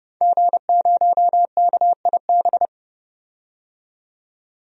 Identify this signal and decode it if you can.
Morse code, heard as a keyed tone, about 32 wpm, 700 Hz. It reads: G0KIB
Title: DXCC-100 Call Signs - 5 Characters - 30wpm